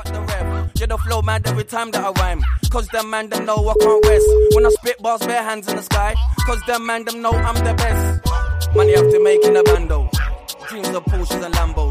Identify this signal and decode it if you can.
Ringback tone (Europe, 425 Hz), with noise.